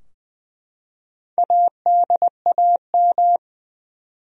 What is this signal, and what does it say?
Morse code, heard as a keyed tone, about 20 wpm, 700 Hz. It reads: ADAM
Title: USA Common Names - Max 4 Letters - 20wpm